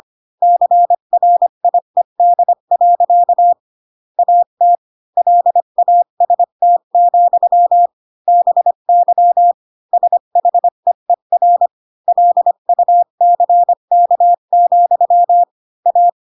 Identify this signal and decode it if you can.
Morse code, heard as a keyed tone, about 25 wpm, 700 Hz. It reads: CRIED. AT LAST, BY SHEER LUCK, A
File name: Pinocchio / Pinocchio ch3 20 25 ik1hge